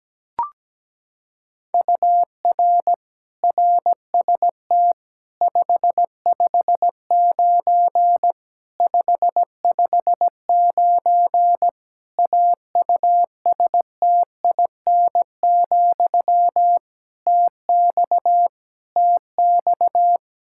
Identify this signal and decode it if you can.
Morse code, heard as a keyed tone, about 17 wpm, 700 Hz. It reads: UR RST 559 559 AUSTIN, TX TX
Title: K1USN SST QSO Practice 17wpm